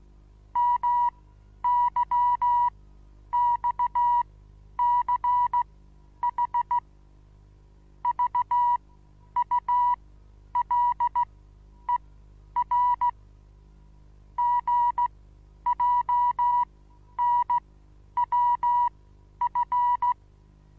Morse code, heard as a keyed tone, about 16 wpm, spaced as for 10 wpm, 970 Hz. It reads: MYXCH VULER GJNWF
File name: Onlinelehrgang / morsen / DJ4UF Morsen25